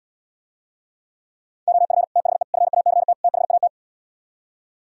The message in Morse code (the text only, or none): OF THE FACT THAT